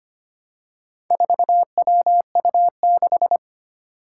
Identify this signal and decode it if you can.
Morse code, heard as a keyed tone, about 25 wpm, 700 Hz. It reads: 4WU6
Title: Four Challenging Letter-Number Combinations 25wpm